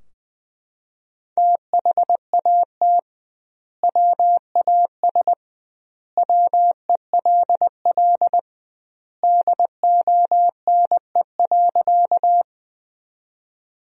Morse code, heard as a keyed tone, about 20 wpm, 700 Hz. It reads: THAT WAS WELL DONE.
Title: 2x Word-Spacing - Sentences from Top 500 Words - 20wpm